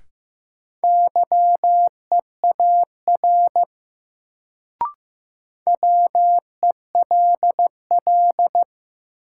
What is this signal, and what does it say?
Morse code, heard as a keyed tone, about 15 wpm, 700 Hz. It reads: YEAR WELL
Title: Sentences from Top 200 Words - New Word Drill - 15wpm